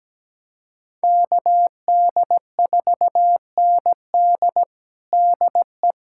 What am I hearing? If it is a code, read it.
Morse code, heard as a keyed tone, about 17 wpm, 700 Hz. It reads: KD4ND DE
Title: USA Callsign DE Callsign - 5 Characters - 17wpm